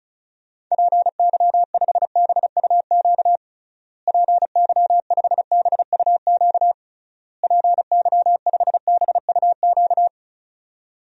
Morse code, heard as a keyed tone, about 36 wpm, 700 Hz. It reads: PY5BUQ PY5BUQ PY5BUQ
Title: DXCC-100 Call Signs - 6 Characters - Repeated 3x - 35wpm